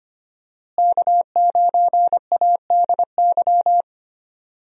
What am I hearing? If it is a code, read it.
Morse code, heard as a keyed tone, about 25 wpm, 700 Hz. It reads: K9ADY
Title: USA Call Signs - 5 Characters - Repeated 3x - 25wpm